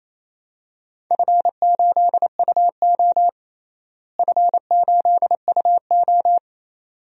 Morse code, heard as a keyed tone, about 28 wpm, 700 Hz. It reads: F8UO F8UO
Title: DXCC-100 Call Signs - 4 Characters - Repeated 2x - 28wpm